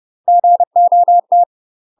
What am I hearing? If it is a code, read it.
Morse code, heard as a keyed tone, about 30 wpm, 700 Hz. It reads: GOT
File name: UK MASTER QSOs TRAINING/30WPM